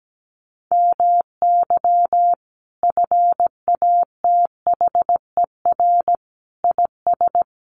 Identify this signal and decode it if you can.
Morse code, heard as a keyed tone, about 17 wpm, 700 Hz. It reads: MY FATHER IS
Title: Sentences from Top 300 Words 17wpm